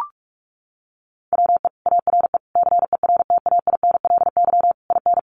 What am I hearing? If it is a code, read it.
Morse code, heard as a keyed tone, about 44 wpm, 700 Hz. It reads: WE ARE CERTAINLY IN
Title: Sentences from Top 1500 Words 45wpm